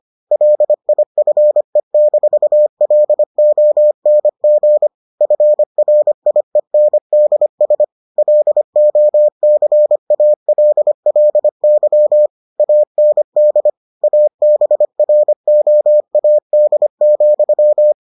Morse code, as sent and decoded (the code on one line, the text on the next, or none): .-.. .. ..-. -....- .-.. --- -. --. / ..-. .-. .. -. -.. ... / .-.. --- -.-. .- .-.. .-.. -.-- / .- -. -.. / .- -... .-. --- .- -.. --..--
LIFE-LONG FRIENDS LOCALLY AND ABROAD,